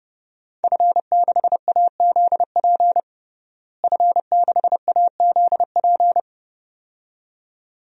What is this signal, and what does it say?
Morse code, heard as a keyed tone, about 30 wpm, 700 Hz. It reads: F6AZP F6AZP
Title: DXCC-100 Call Signs - 5 Characters - Repeated 2x - 30wpm